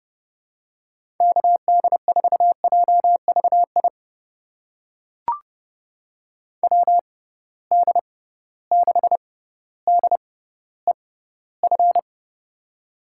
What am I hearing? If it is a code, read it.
Morse code, heard as a keyed tone, about 29 wpm, spaced as for 15 wpm, 700 Hz. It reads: KD4JVI WD6DEF
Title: USA Call Signs - 6 Characters - 6x Character Spacing - 30wpm